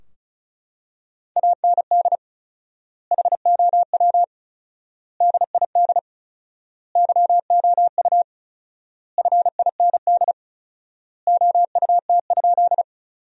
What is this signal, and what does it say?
Morse code, heard as a keyed tone, about 36 wpm, 700 Hz. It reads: AND HOW DID YOU FIND OUT?
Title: 4x Word-Spacing - Sentences from Top 200 Words - 35wpm